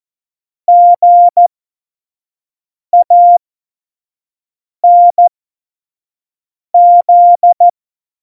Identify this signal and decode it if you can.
Morse code, heard as a keyed tone, about 14 wpm, spaced as for 5 wpm, 700 Hz. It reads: GANZ
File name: Morsekurs / Witze / Witz 24.Dezember 2019, 25 F70